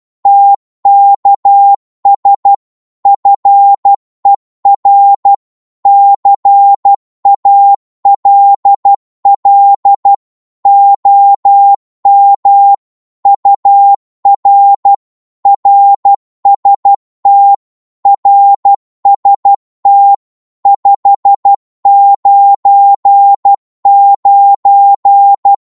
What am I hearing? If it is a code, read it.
Morse code, heard as a keyed tone, about 12 wpm, 800 Hz. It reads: TKS FER CALL OM UR RST RST 599
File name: Morse / qso-12wpm